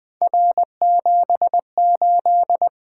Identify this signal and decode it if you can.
Morse code, heard as a keyed tone, about 20 wpm, 700 Hz. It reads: R78
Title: Three Letter-Number Combinations - 20wpm